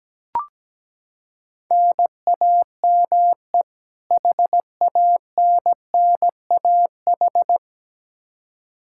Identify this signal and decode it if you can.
Morse code, heard as a keyed tone, about 17 wpm, 700 Hz. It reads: NAME HANNAH
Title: Practice QSO - Exchange 1 - Beginning - 17wpm